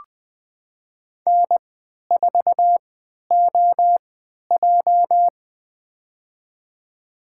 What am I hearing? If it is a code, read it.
Morse code, heard as a keyed tone, about 20 wpm, spaced as for 11 wpm, 700 Hz. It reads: N4OJ
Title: USA Call Signs - 3x Character Spacing - 20wpm